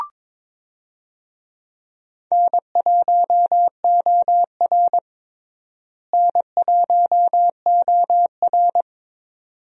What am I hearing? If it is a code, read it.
Morse code, heard as a keyed tone, about 22 wpm, 700 Hz. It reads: N1OR N1OR
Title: USA Call Signs - 4 Characters - Repeated 2x - 22wpm